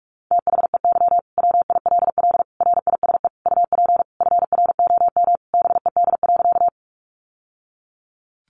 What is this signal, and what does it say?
Morse code, heard as a keyed tone, about 44 wpm, 700 Hz. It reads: THEY WILL RISE UP FROM BED.
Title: Sentences from Top 900 Words 45wpm